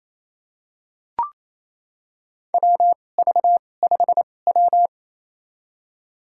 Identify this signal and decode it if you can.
Morse code, heard as a keyed tone, about 28 wpm, spaced as for 20 wpm, 700 Hz. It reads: WV5W
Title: USA Call Signs - 4 Characters - 2x Character Spacing - 28wpm